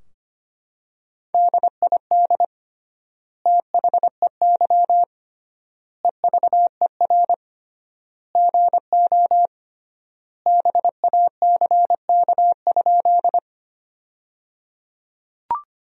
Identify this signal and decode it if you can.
Morse code, heard as a keyed tone, about 25 wpm, 700 Hz. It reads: DID THEY EVER GO BACK?
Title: 3x Word-Spacing - Sentences from Top 400 Words - 25wpm